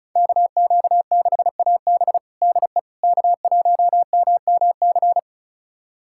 Morse code, heard as a keyed tone, about 34 wpm, 700 Hz. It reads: KQ6AB DE K1MMC
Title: USA Callsign DE Callsign - 5 Characters - 35wpm